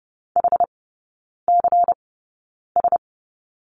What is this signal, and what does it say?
Morse code, heard as a keyed tone, about 30 wpm, spaced as for 9 wpm, 700 Hz. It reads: HCS
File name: Lesson 16e - Letter c - Three Characters 30wpm